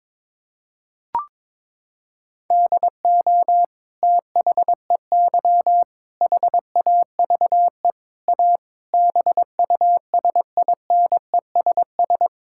Morse code, heard as a keyed tone, about 21 wpm, 700 Hz. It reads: DO THEY HAVE A BUSINESS
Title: Sentences from Top 1000 Words - Review Part 5 - 22wpm